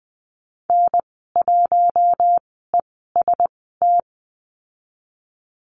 Morse code, heard as a keyed tone, about 20 wpm, spaced as for 14 wpm, 700 Hz. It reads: N1EST
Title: 082 Lesson 14g - Number 2 - Call Signs 20wpm